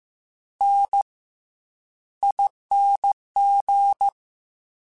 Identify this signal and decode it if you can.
Morse code, heard as a keyed tone, about 15 wpm, 800 Hz. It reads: N ING